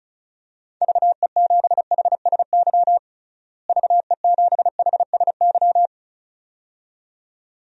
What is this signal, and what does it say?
Morse code, heard as a keyed tone, about 34 wpm, 700 Hz. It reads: VE7HSY VE7HSY
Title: DXCC-100 Call Signs - 6 Characters - Repeated 2x - 35wpm